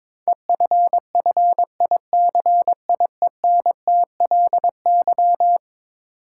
Morse code, heard as a keyed tone, about 22 wpm, 700 Hz. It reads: EFFICIENTLY
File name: practice / 22wpm / Eleven Letter Words - 22wpm